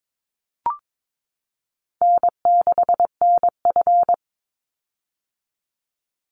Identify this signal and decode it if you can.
Morse code, heard as a keyed tone, about 22 wpm, 700 Hz. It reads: N6NF
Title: Easy Call Signs - 22wpm